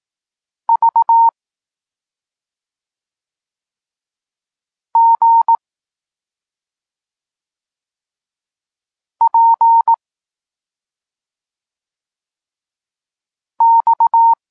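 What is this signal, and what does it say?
Morse code, heard as a keyed tone, about 18 wpm, 920 Hz. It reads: V G P X
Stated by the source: Polish Morse lesson teaching the letter C, from SP9MOA Niepołomicki Klub Krótkofalowców.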